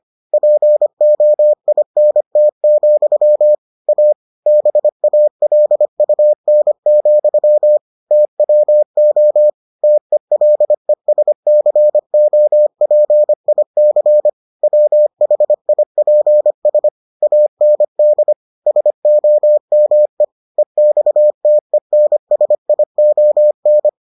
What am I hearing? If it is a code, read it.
Morse code, heard as a keyed tone, about 25 wpm, 600 Hz. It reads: POINT, A BALUN, TWO TELESCOPIC WHIPS AND SOME EXTENSION